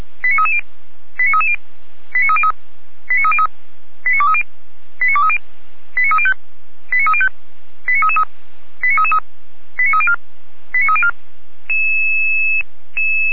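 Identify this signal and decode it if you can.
Five-tone selective call (ZVEI), heard as quick runs of five tones, tones of about 70 ms. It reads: 89330 89330 89303 89230 89230 89305 89305 89333 89304 89304
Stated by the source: ZVEI five-tone call received by SDR, recorded by Feuerwehr Orlamünde.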